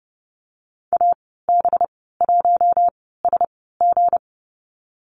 Morse code, heard as a keyed tone, about 30 wpm, spaced as for 17 wpm, 700 Hz. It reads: AB1SG
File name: Lesson 37g - Letter x - Call Signs 30wpm